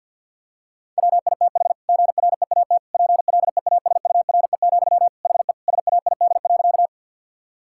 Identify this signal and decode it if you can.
Morse code, heard as a keyed tone, about 50 wpm, 700 Hz. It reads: WITH GREAT PLEASURE, HE SAID.
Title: Sentences from Top 1600 Words 50wpm